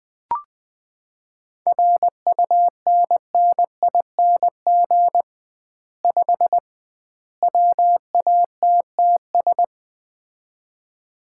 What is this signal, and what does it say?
Morse code, heard as a keyed tone, about 20 wpm, 700 Hz. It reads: RUNNING 5 WATTS
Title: Common QSO Phrases - 2x Word-Spacing - 20wpm